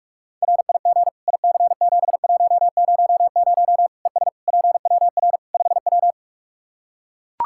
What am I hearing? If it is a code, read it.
Morse code, heard as a keyed tone, about 41 wpm, 700 Hz. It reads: RIG IC7100 ES PWR 5W
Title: Practice QSO - Exchange 2 - Info - 45wpm